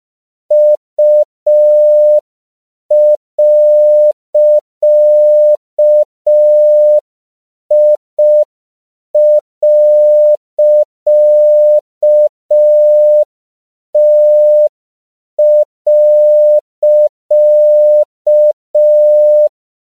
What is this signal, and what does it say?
Morse code, heard as a keyed tone, about 5 wpm, 600 Hz. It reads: U.I.T.
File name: CW / 5PPM